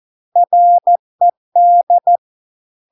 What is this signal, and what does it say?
Morse code, heard as a keyed tone, about 14 wpm, 700 Hz. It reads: RED